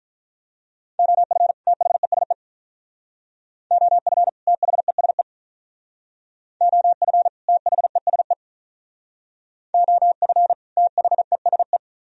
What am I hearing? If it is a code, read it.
Morse code, heard as a keyed tone, about 46 wpm, 700 Hz. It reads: OF THESE OF THESE OF THESE OF THESE